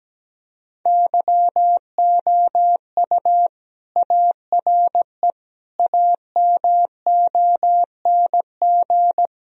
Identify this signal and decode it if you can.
Morse code, heard as a keyed tone, about 17 wpm, 700 Hz. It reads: YOU ARE AMONG
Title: Sentences from Top 1000 Words - Review Part 3 - 17wpm